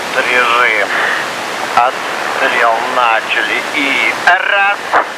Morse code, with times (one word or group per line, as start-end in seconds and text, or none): none